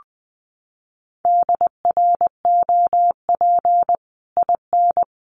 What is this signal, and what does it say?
Morse code, heard as a keyed tone, about 20 wpm, 700 Hz. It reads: DROP IN
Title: Sentences from Top 1400 Words 20wpm